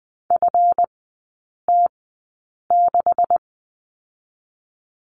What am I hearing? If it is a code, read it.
Morse code, heard as a keyed tone, about 20 wpm, spaced as for 8 wpm, 700 Hz. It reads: FT6
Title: 138 Lesson 23e - Letter f - Three Characters 20wpm